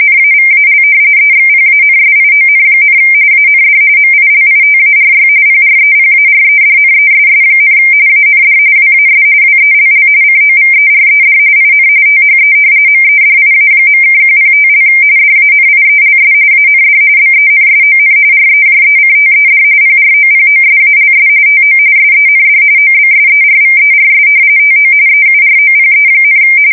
RTTY, about 45.45 baud. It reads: NAVAL EXPANSION, AND MILITARY PREPAREDNESS, AND WAS WELL COVERED IN THE MEDIA.(105) DANIELS AND BENSON REJECTED PROPOSALS TO SEND OBSERVERS TO EUROPE,
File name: fsk170 / usnavy4